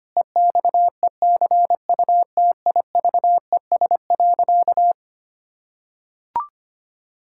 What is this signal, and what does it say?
Morse code, heard as a keyed tone, about 25 wpm, 700 Hz. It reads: EXECUTIVES.
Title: Pangram Challenge v2 - 25wpm